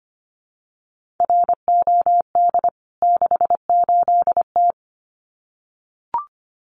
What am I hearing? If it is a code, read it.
Morse code, heard as a keyed tone, about 25 wpm, 700 Hz. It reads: ROD 68T
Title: CWT Exchange 25wpm